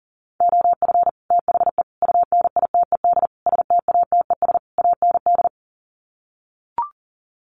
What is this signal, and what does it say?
Morse code, heard as a keyed tone, about 40 wpm, 700 Hz. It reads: OF THE UNITED STATES AND
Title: Sets of 5 Words 40wpm